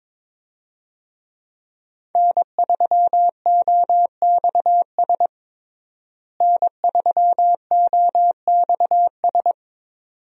Morse code, heard as a keyed tone, about 22 wpm, 700 Hz. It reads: N3OXS N3OXS
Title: USA Call Signs - 5 Characters - Repeated 2x - 22wpm